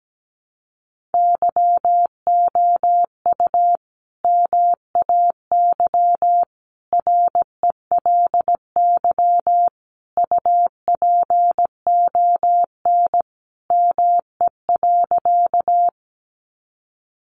Morse code, heard as a keyed tone, about 17 wpm, 700 Hz. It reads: YOU MAY RELY UPON ME.
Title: Sentences from Top 1200 Words 17wpm